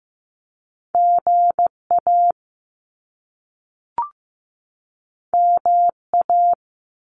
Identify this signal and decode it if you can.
Morse code, heard as a keyed tone, about 15 wpm, 700 Hz. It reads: GA MA
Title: US States 15wpm